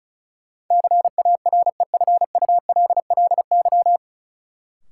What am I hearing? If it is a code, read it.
Morse code, heard as a keyed tone, about 36 wpm, 700 Hz. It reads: CAREFULLY
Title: Sentences from Top 1300 Words - New Word Drill - 35wpm